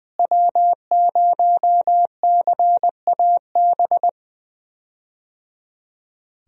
Morse code, heard as a keyed tone, about 20 wpm, 700 Hz. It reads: W0CAB
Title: USA Call Signs - 5 Characters - Repeated 3x - 20wpm